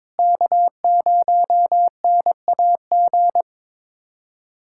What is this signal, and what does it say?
Morse code, heard as a keyed tone, about 21 wpm, 700 Hz. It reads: K0NAG
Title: USA Callsign DE Callsign - 5 Characters - 22wpm